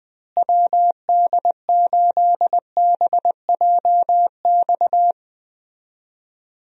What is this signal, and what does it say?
Morse code, heard as a keyed tone, about 21 wpm, 700 Hz. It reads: WD8BJX
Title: USA Call Signs - 6 Characters - Repeated 3x - 20wpm